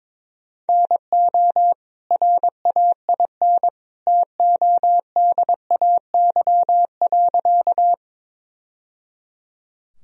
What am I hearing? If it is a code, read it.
Morse code, heard as a keyed tone, about 22 wpm, 700 Hz. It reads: NO RAIN TODAY.
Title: Sentences from Top 1000 Words - Review Part 1 - 22wpm